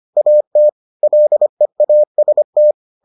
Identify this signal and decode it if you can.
Morse code, heard as a keyed tone, about 25 wpm, 600 Hz. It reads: AT LEAST